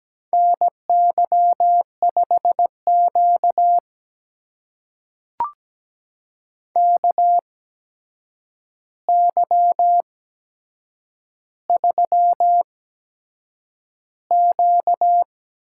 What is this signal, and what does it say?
Morse code, heard as a keyed tone, about 17 wpm, spaced as for 7 wpm, 700 Hz. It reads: NY5Q KY3Q
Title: USA Call Signs - 4 Characters - 8x Character Spacing - 17wpm